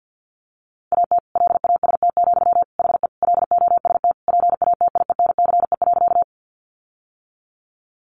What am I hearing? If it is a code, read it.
Morse code, heard as a keyed tone, about 50 wpm, 700 Hz. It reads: AT LAST, HE LOST PATIENCE.
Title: Sentences from Top 1400 Words 50wpm